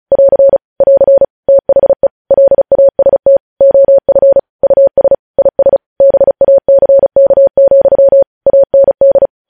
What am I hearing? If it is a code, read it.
Morse code, heard as a keyed tone, about 36 wpm, 550 Hz. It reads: <AR> <AR> THE LAST OF US IS BACK, AND
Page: news headlines in morse code at 35 wpm